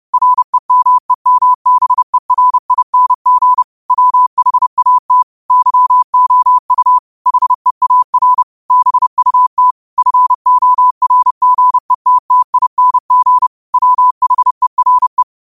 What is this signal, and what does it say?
Morse code, heard as a keyed tone, about 30 wpm, 1000 Hz. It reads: REMEMBERING WHAT YOU HEAR BUT FORGETTING WHERE
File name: qotdfiles / 30wpm